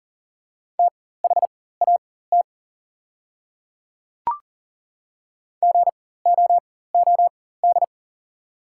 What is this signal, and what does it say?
Morse code, heard as a keyed tone, about 40 wpm, spaced as for 19 wpm, 690 Hz. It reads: THAT GOOD
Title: 4x Character Spacing - Top 300 Words - 40wpm